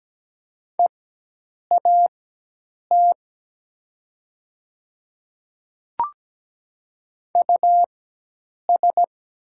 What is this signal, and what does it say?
Morse code, heard as a keyed tone, about 17 wpm, spaced as for 8 wpm, 700 Hz. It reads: EAT US
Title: 4x Character Spacing - Top 300 Words - 17wpm